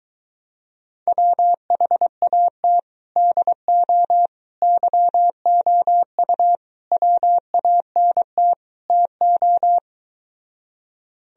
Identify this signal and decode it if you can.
Morse code, heard as a keyed tone, about 23 wpm, 700 Hz. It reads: WHAT DO YOU WANT TO